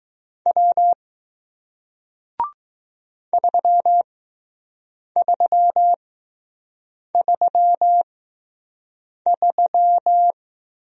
Morse code, heard as a keyed tone, about 20 wpm, spaced as for 6 wpm, 700 Hz. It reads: W 3333